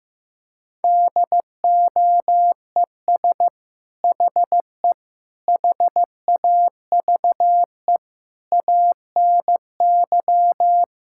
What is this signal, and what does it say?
Morse code, heard as a keyed tone, about 15 wpm, 700 Hz. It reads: DOES HE HAVE ANY
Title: Sentences from Top 1600 Words 15wpm